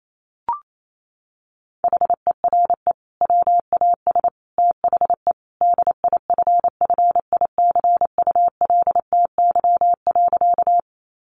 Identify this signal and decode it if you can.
Morse code, heard as a keyed tone, about 28 wpm, 700 Hz. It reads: HERE WAS THE DIFFICULTY.
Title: Sentences from Top 1500 Words 28wpm